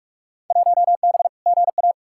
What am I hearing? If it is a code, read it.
Morse code, heard as a keyed tone, about 42 wpm, 700 Hz. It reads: 1B GA
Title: Field Day Exchanges - 45wpm